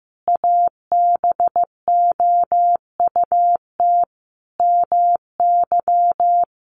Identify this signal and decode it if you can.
Morse code, heard as a keyed tone, about 15 wpm, 700 Hz. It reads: ABOUT MY
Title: Sentences from Top 500 Words 15wpm